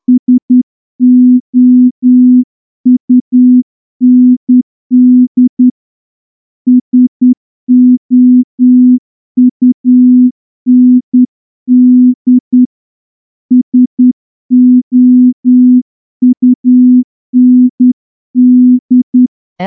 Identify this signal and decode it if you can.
Morse code, heard as a keyed tone, about 10 wpm, 250 Hz. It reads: SOUND SOUND SOUND
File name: morse-training-10WPM